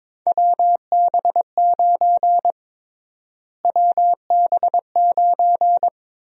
Morse code, heard as a keyed tone, about 23 wpm, 700 Hz. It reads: WB9 WB9
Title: USA Call Sign Prefixes - Repeated 2x - 22wpm